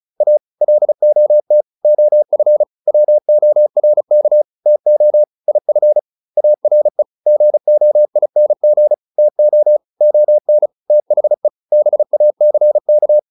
Morse code, heard as a keyed tone, about 34 wpm, 600 Hz. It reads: A LOT OF WORK TO IF ARE GOING TO ON THE BACK